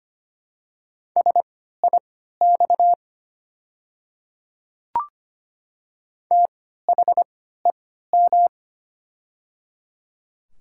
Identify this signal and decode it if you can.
Morse code, heard as a keyed tone, about 25 wpm, spaced as for 14 wpm, 700 Hz. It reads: SIX THEM
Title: 3x Character Spacing - Top 500 Words - 25wpm